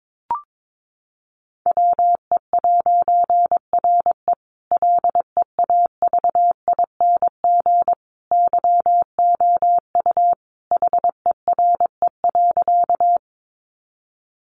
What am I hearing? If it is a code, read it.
Morse code, heard as a keyed tone, about 22 wpm, 700 Hz. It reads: WE'RE LEAVING YOU HERE.
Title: A Sound Of Thunder 22wpm